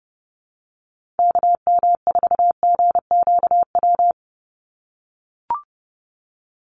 Morse code, heard as a keyed tone, about 30 wpm, 700 Hz. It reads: KM4GQW
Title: USA Call Signs 30wpm